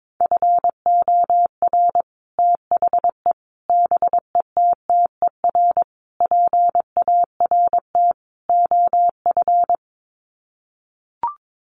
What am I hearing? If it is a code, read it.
Morse code, heard as a keyed tone, about 22 wpm, 700 Hz. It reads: FOR THE BETTER PART OF
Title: Sets of 5 Words Encore2 22wpm